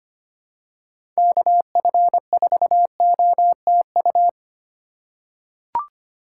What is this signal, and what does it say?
Morse code, heard as a keyed tone, about 25 wpm, 700 Hz. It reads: KF4OTU
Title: USA Call Signs - 6 Characters - 4x Character Spacing - 25wpm